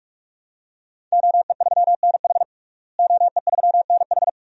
Morse code, heard as a keyed tone, about 46 wpm, 700 Hz. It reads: OE3NH OE3NH
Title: DXCC-100 Call Signs - 5 Characters - Repeated 2x - 45wpm